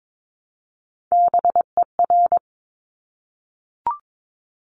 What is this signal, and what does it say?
Morse code, heard as a keyed tone, about 22 wpm, 700 Hz. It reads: BER
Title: Top Three Letter Combinations in Words 22wpm